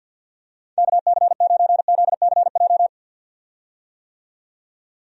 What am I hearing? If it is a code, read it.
Morse code, heard as a keyed tone, about 52 wpm, 700 Hz. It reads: KC9ZCJ
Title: USA Call Signs - 6 Characters - 50wpm